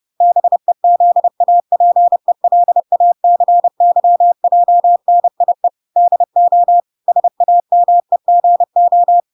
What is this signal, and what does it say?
Morse code, heard as a keyed tone, about 30 wpm, 700 Hz. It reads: BEZAPELACYJNIE DO SAMEGO
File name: recordings / 30wpm0000